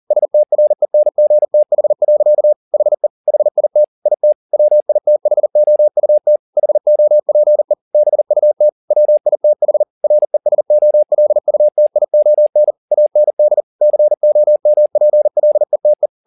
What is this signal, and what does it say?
Morse code, heard as a keyed tone, about 40 wpm, 600 Hz. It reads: STRENGTH. HE HIT IT WITHOUT HOPE BUT WITH RESOLUTION AND COMPLETE
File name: the old man and the sea 2 40wpm